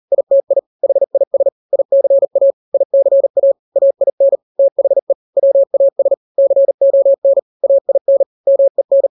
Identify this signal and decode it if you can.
Morse code, heard as a keyed tone, about 40 wpm, 550 Hz. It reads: ITI HIS ICA ICA AIN THE WAS CON AIN MEN
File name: n-grams / 3-grams 40 35 WPM